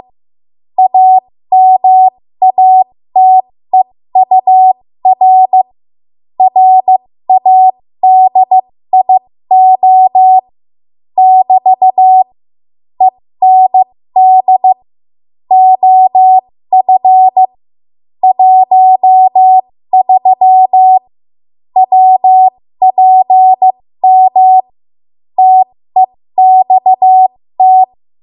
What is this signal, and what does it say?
Morse code, heard as a keyed tone, about 15 wpm, spaced as for 13 wpm, 750 Hz. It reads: AMATEUR RADIO <BT> END OF 13 WPM TEXT
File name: Morse / 13WPM